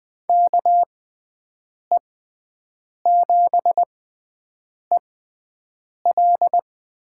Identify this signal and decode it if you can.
Morse code, heard as a keyed tone, about 20 wpm, spaced as for 7 wpm, 700 Hz. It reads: KE7EL
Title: USA Call Signs - 5 Characters - 6x Character Spacing - 20wpm